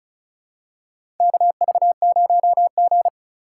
Morse code, heard as a keyed tone, about 36 wpm, 700 Hz. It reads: KV0G
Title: USA Call Signs - 4 Characters - 3x Character Spacing - 35wpm